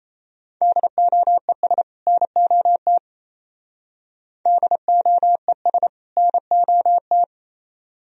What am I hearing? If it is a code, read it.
Morse code, heard as a keyed tone, about 32 wpm, 700 Hz. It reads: DOES NOT DOES NOT
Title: Speed-Racing - Sets of 2 Words 28wpm